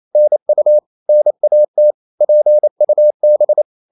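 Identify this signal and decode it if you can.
Morse code, heard as a keyed tone, about 28 wpm, 600 Hz. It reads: NU NAT PUB
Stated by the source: Danish short words sent in Morse at 28 wpm, from CW med Gnister.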